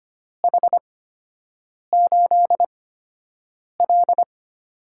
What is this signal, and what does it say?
Morse code, heard as a keyed tone, about 25 wpm, spaced as for 7 wpm, 700 Hz. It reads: H8L
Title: USA Call Signs - 4 Characters - 8x Character Spacing - 25wpm